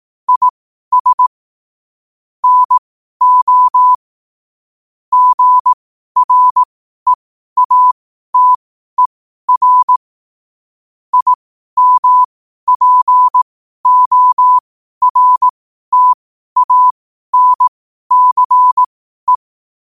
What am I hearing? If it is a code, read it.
Morse code, heard as a keyed tone, about 18 wpm, spaced as for 12 wpm, 1000 Hz. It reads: IS NO GREATER IMPORTANCE
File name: qotdfiles / 12wpm